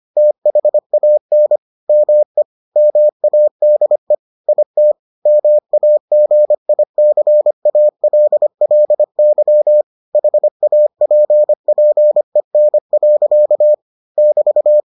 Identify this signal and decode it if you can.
Morse code, heard as a keyed tone, about 25 wpm, 600 Hz. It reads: THAN ME MADE IT MAGICALLY HAPPEN. <BT>